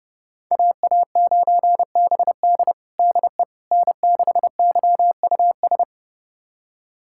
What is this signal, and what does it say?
Morse code, heard as a keyed tone, about 30 wpm, 700 Hz. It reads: AA9BD DE N6YUS
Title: USA Callsign DE Callsign 30wpm